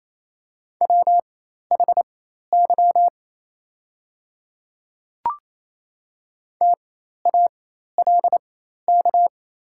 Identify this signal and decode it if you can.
Morse code, heard as a keyed tone, about 28 wpm, spaced as for 13 wpm, 700 Hz. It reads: WHY TALK
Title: 4x Character Spacing - Top 500 Words - 28wpm